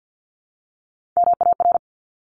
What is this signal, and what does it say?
Morse code, heard as a keyed tone, about 50 wpm, 700 Hz. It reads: MAR
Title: ARRL Sections 50wpm